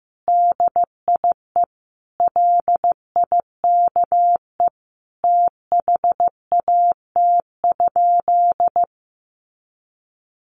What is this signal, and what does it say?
Morse code, heard as a keyed tone, about 15 wpm, 700 Hz. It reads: DIE LIKE THAT?
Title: Sentences from Top 1400 Words 15wpm